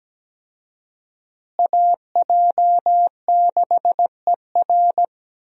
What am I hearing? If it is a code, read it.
Morse code, heard as a keyed tone, about 17 wpm, 700 Hz. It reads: AJ6ER
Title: USA Call Signs - 5 Characters - Repeated 3x - 17wpm